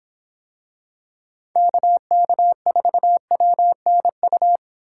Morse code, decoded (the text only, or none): KK4WNU